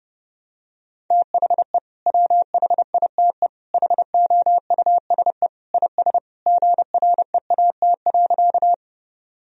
Morse code, heard as a keyed tone, about 30 wpm, 700 Hz. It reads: THE WHITE HOUSE IS GREAT.
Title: Sentences from Top 1000 Words - Review Part 1 - 30wpm